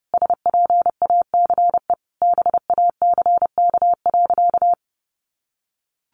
Morse code, heard as a keyed tone, about 30 wpm, 700 Hz. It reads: SPACE BACK.
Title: Sentences from Top 1500 Words 30wpm